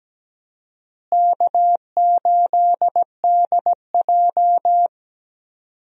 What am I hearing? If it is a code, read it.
Morse code, heard as a keyed tone, about 17 wpm, 700 Hz. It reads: K8DJ
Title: USA Call Signs - 4 Characters - Repeated 2x - 17wpm